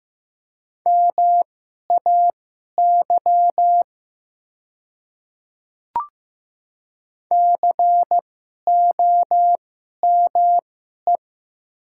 Morse code, heard as a keyed tone, about 15 wpm, spaced as for 11 wpm, 700 Hz. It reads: MAY COME
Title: 2x Character Spacing - Top 200 Words - 15wpm